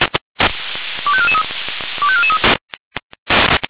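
Five-tone selective call (ZVEI), heard as quick runs of five tones, tones of about 70 ms. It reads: 25443 25443
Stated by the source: Five-tone ZVEI selective call received on SDR.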